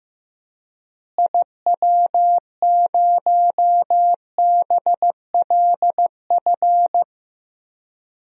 Morse code, heard as a keyed tone, about 16 wpm, 700 Hz. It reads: IW0BLF
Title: DXCC-100 Call Signs - 6 Characters - Repeated 3x - 15wpm